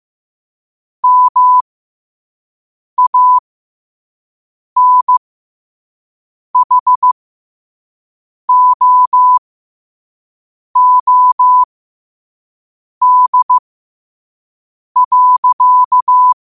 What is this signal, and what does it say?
Morse code, heard as a keyed tone, about 15 wpm, spaced as for 5 wpm, 1000 Hz. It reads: MANHOOD.
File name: qotdfiles / 5wpm